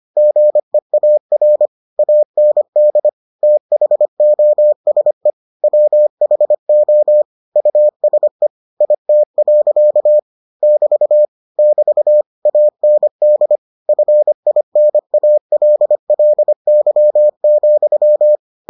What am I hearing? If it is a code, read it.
Morse code, heard as a keyed tone, about 25 wpm, 600 Hz. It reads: GEAR AND THOSE WHO USE IT. <BT> <BT> AND FINALLY,